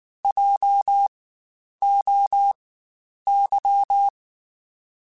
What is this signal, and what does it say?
Morse code, heard as a keyed tone, about 19 wpm, spaced as for 9 wpm, 800 Hz. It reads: JOY
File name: Spaced Alphas / Spaced Alphas mp3s / Js sp @18